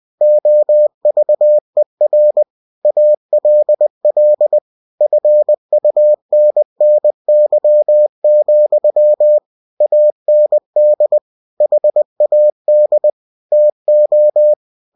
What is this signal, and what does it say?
Morse code, heard as a keyed tone, about 20 wpm, 600 Hz. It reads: OVER ALL FUNNY, AND HAD TO
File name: cw-books / winnie-the-pooh / winnie-20wpm-page-0032